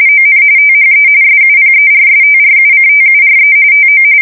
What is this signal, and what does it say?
RTTY, about 45.45 baud. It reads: ON THE SUPERSTRUCTURE TO
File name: fsk170 / battleshipnj